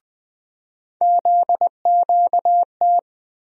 Morse code, heard as a keyed tone, about 20 wpm, 700 Hz. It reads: ZQT
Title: USA Call Sign Postfixes - 3 Letters 20wpm